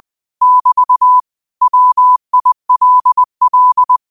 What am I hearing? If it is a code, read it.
Morse code, heard as a keyed tone, about 20 wpm, 1000 Hz. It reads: <BT> WILL
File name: qotdfiles / 20wpm